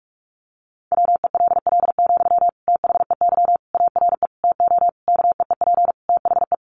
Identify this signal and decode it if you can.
Morse code, heard as a keyed tone, about 44 wpm, 700 Hz. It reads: WELL, THEY ARE TO KEEP THE
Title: Sentences from Top 1500 Words 45wpm